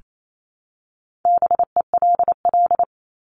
Morse code, heard as a keyed tone, about 28 wpm, 700 Hz. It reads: BELL
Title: Top 1751-2000 Words 28wpm